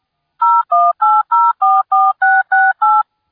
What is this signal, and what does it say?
DTMF, heard as telephone touch tones, tones of about 220 ms. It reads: 018044668